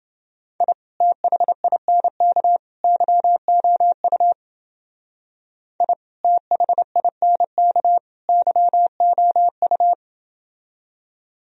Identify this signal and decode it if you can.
Morse code, heard as a keyed tone, about 28 wpm, 700 Hz. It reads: I THINK YOU I THINK YOU